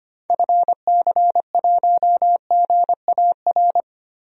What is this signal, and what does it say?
Morse code, heard as a keyed tone, about 25 wpm, 700 Hz. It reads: FC1GAR
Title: DXCC-100 Call Signs - 6 Characters - Repeated 3x - 25wpm